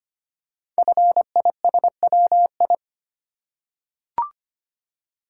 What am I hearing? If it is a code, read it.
Morse code, heard as a keyed tone, about 25 wpm, 700 Hz. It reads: FISWI
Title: Code Groups - 5 Letters - 25wpm